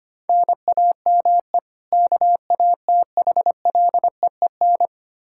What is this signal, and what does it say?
Morse code, heard as a keyed tone, about 25 wpm, 700 Hz. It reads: NAME KATHLEEN
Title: Practice QSO - Exchange 1 - Beginning - 25wpm